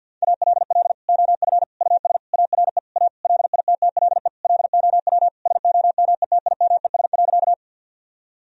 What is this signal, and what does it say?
Morse code, heard as a keyed tone, about 50 wpm, 700 Hz. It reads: ALL OF US ARE A LITTLE LOW SOMETIMES.
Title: Sentences from Top 1000 Words - Review Part 3 - 50wpm